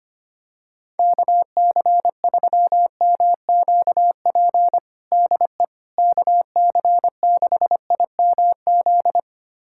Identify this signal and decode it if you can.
Morse code, heard as a keyed tone, about 25 wpm, 700 Hz. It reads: KC3MQP DE KC6IMZ
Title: USA Callsign DE Callsign 25wpm